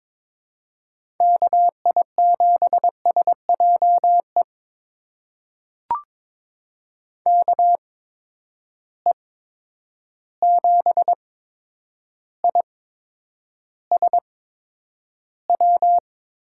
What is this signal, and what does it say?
Morse code, heard as a keyed tone, about 22 wpm, spaced as for 9 wpm, 700 Hz. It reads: KI7SJE KE7ISW
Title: USA Call Signs - 6 Characters - 8x Character Spacing - 22wpm